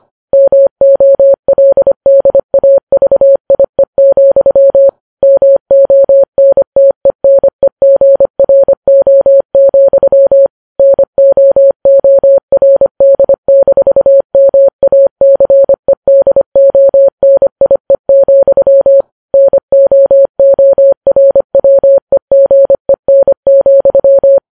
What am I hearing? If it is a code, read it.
Morse code, heard as a keyed tone, about 25 wpm, 550 Hz. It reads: MOLDAVIE, MONTENEGRO, NOORD-MACEDONIE, NOORWEGEN,
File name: Files Web tls / Morse on9fv / morse25kps